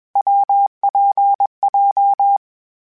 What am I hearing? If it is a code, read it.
Morse code, heard as a keyed tone, about 21 wpm, 800 Hz. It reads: WPJ
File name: Variable Speed / Variable Speed-mp3s / WPJ WPJ1 sp @20